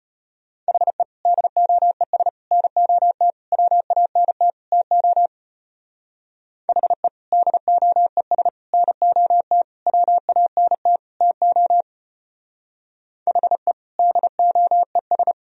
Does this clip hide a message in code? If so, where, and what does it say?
Yes: Morse code from 0.7 to 15.3 s, heard as a keyed tone, about 33 wpm, 700 Hz. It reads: HE DOES NOT WANT TO HE DOES NOT WANT TO HE DOES